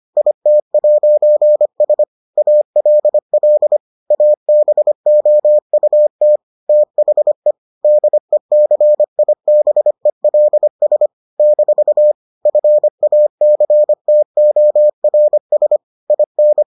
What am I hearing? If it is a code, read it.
Morse code, heard as a keyed tone, about 25 wpm, 600 Hz. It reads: IT'S ALL ABOUT THE DECIBELS - FACTORS IN